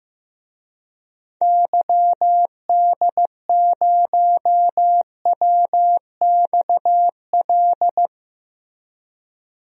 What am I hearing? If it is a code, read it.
Morse code, heard as a keyed tone, about 15 wpm, 700 Hz. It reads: YD0WXL
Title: DXCC-100 Call Signs - 6 Characters - Repeated 2x - 15wpm